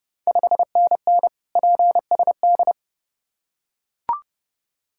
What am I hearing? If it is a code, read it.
Morse code, heard as a keyed tone, about 28 wpm, 690 Hz. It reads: 5NN PSD
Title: Ontario QSO Party Exchange 30wpm